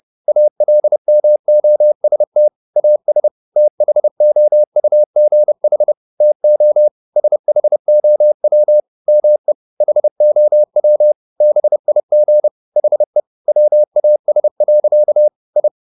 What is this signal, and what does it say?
Morse code, heard as a keyed tone, about 30 wpm, 600 Hz. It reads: ALMOST AS THOUGH TO SHOW ME HOW BIG HE WAS. I
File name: the old man and the sea / the old man and the sea 1 30wpm